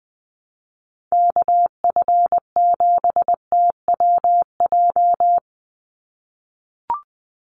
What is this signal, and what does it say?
Morse code, heard as a keyed tone, about 20 wpm, 700 Hz. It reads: KF7TWJ
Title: USA Call Signs 20wpm